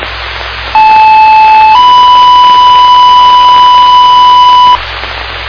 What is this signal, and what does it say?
Two-tone page, heard as a one-second tone, then a three-second tone, about 870 then 990 Hz, with noise. It is heard from 0.7 to 4.8 s.
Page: Indian Lake Fire page received at 19:01:19 on 01/31/17